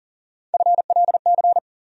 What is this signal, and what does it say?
Morse code, heard as a keyed tone, about 40 wpm, 700 Hz. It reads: FLC
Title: USA Call Sign Postfixes - 3 Letters 40wpm